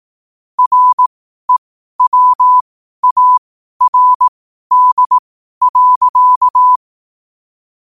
Morse code, heard as a keyed tone, about 18 wpm, spaced as for 12 wpm, 1000 Hz. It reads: REWARD.
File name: qotdfiles / 12wpm